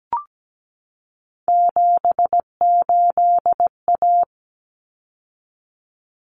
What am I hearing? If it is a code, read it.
Morse code, heard as a keyed tone, about 17 wpm, 690 Hz. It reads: 78A
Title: Common Cut Numbers - 3-Digit 17wpm